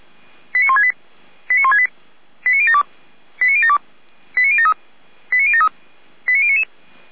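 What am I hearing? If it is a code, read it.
Five-tone selective call (ZVEI), heard as quick runs of five tones, tones of about 70 ms. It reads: 89168 89168 89072 89072 89073 89073 89099
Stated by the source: ZVEI five-tone selective call received on SDR.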